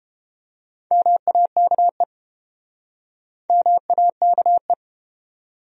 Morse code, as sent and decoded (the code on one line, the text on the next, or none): -- .- -.- . / -- .- -.- .
MAKE MAKE